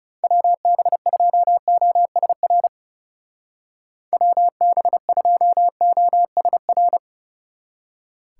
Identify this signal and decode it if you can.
Morse code, heard as a keyed tone, about 39 wpm, 700 Hz. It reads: WB2OSR WB2OSR